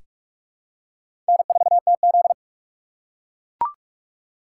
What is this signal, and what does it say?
Morse code, heard as a keyed tone, about 48 wpm, 700 Hz. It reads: N4TZ
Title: K1USN SST QSO Practice - 45wpm